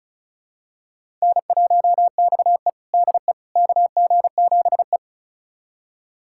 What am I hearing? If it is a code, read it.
Morse code, heard as a keyed tone, about 36 wpm, 700 Hz. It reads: N1XE DE KG7E
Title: USA Callsign DE Callsign - 4 Characters - 35wpm